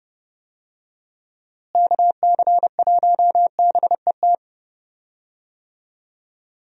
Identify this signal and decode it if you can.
Morse code, heard as a keyed tone, about 30 wpm, 700 Hz. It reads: KC1BET
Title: USA Call Signs - 6 Characters - Repeated 2x - 30wpm